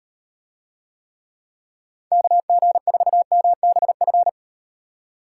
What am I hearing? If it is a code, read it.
Morse code, heard as a keyed tone, about 38 wpm, 700 Hz. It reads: KG4MBF